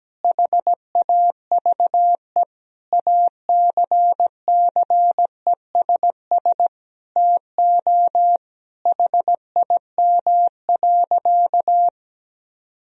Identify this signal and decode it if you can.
Morse code, heard as a keyed tone, about 17 wpm, 700 Hz. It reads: HAVE ACCESS TO HIM.
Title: Sentences from Top 1000 Words - Review Part 4 - 17wpm